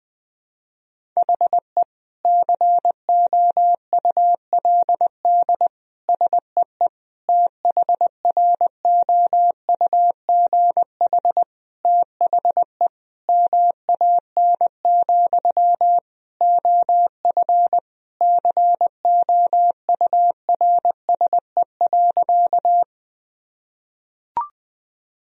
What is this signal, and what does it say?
Morse code, heard as a keyed tone, about 20 wpm, 700 Hz. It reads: HE COULD SEE THROUGH THE MAN, OF COURSE.
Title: Sentences from Top 1000 Words - Review Part 1 - 20wpm